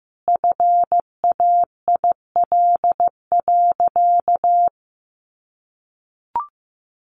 Morse code, heard as a keyed tone, about 15 wpm, 700 Hz. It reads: FAIL.
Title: Sentences from Top 1500 Words 15wpm